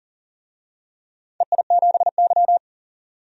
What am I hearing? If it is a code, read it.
Morse code, heard as a keyed tone, about 40 wpm, 700 Hz. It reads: EI7Y
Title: DXCC-100 Call Signs - 4 Characters - 40wpm